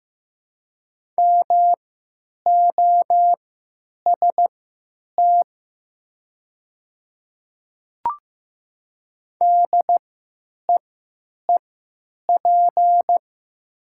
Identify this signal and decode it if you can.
Morse code, heard as a keyed tone, about 15 wpm, spaced as for 9 wpm, 700 Hz. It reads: MOST DEEP
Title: 3x Character Spacing - Top 500 Words - 15wpm